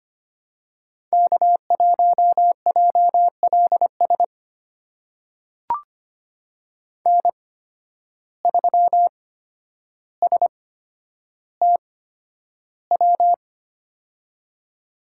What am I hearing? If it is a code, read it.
Morse code, heard as a keyed tone, about 25 wpm, spaced as for 11 wpm, 700 Hz. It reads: K1JLS N3STW
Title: USA Call Signs - 5 Characters - 8x Character Spacing - 25wpm